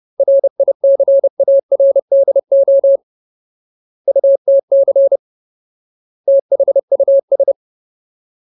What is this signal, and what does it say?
Morse code, heard as a keyed tone, about 30 wpm, 550 Hz. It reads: RICARDO UTC THUS
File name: Ultimate Word List / 30WPM Ultimate